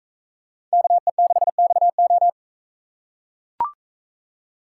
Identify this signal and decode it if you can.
Morse code, heard as a keyed tone, about 41 wpm, 700 Hz. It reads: KE6XO